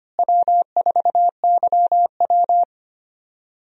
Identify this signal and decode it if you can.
Morse code, heard as a keyed tone, about 25 wpm, 700 Hz. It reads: W4YW
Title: USA Call Signs - 4 Characters - Repeated 2x - 25wpm